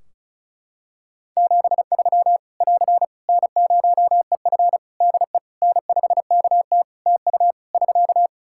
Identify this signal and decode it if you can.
Morse code, heard as a keyed tone, about 36 wpm, 700 Hz. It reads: 73 <AR> N0EF DE N5KT TU <SK>
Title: Practice QSO - Exchange 4 - Ending - 35wpm